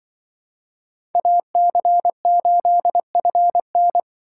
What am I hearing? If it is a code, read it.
Morse code, heard as a keyed tone, about 25 wpm, 700 Hz. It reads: AC8FN
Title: Speed Racing - USA 2x2 Call Signs 20wpm